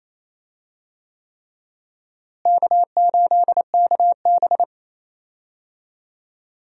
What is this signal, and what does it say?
Morse code, heard as a keyed tone, about 28 wpm, 700 Hz. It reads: K8KB
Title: USA Call Signs - 4 Characters - Repeated 3x - 28wpm